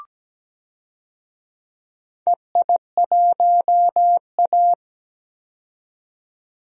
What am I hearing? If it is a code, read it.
Morse code, heard as a keyed tone, about 17 wpm, 700 Hz. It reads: EI1A
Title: DXCC-100 Call Signs - 4 Characters - 17wpm